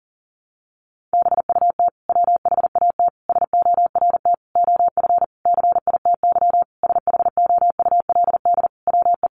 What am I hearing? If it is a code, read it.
Morse code, heard as a keyed tone, about 40 wpm, 700 Hz. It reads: BUT WHAT SORT OF CITY SHOULD WE
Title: Sentences from Top 1300 Words 40wpm